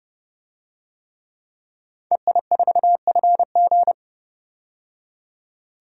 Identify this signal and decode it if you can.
Morse code, heard as a keyed tone, about 30 wpm, 700 Hz. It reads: EI4FG
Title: DXCC-100 Call Signs - 5 Characters - 30wpm